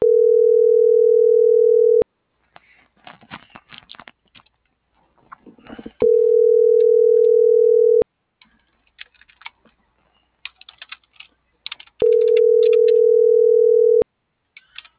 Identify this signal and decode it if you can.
Ringback tone (North America), 440 + 480 Hz.